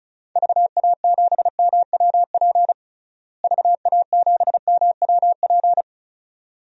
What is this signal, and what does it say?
Morse code, heard as a keyed tone, about 35 wpm, 700 Hz. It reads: VA7MWP VA7MWP
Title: DXCC-100 Call Signs - 6 Characters - Repeated 2x - 35wpm